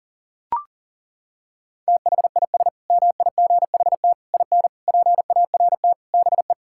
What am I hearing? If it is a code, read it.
Morse code, heard as a keyed tone, about 40 wpm, 700 Hz. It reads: THIS MIGHT IN PART BE
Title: Sentences from Top 1000 Words - Review Part 5 - 40wpm